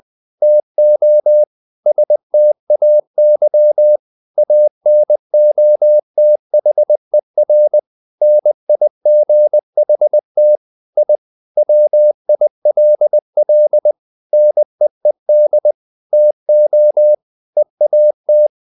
Morse code, heard as a keyed tone, about 20 wpm, 600 Hz. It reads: TO STAY ANOTHER NIGHT I WILL NEED TO EAT
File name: the old man and the sea / the old man and the sea part 1 20wpm